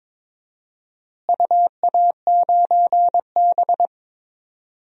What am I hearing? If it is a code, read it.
Morse code, heard as a keyed tone, about 22 wpm, 700 Hz. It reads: UA9B
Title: DXCC-100 Call Signs - 4 Characters - Repeated 2x - 22wpm